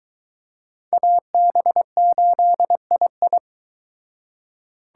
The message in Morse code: AB8II

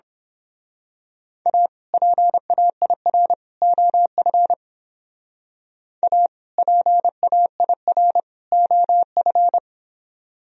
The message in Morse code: A PAIR OF A PAIR OF

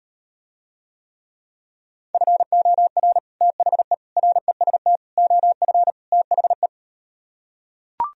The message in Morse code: FOR THE REST OF THE

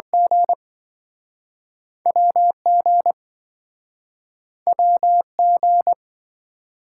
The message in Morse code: G WG WG

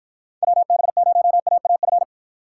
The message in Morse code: WB0AAF

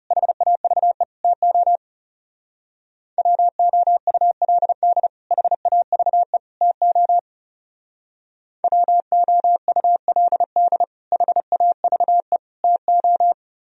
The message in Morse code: HAVE TO WOULD HAVE TO WOULD HAVE TO

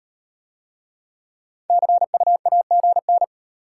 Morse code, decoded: CU AGN

